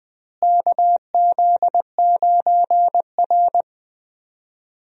Morse code, heard as a keyed tone, about 20 wpm, 700 Hz. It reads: KZ9R
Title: USA Call Signs - 4 Characters - Repeated 3x - 20wpm